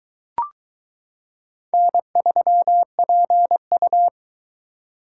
Morse code, heard as a keyed tone, about 22 wpm, 700 Hz. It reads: N3PU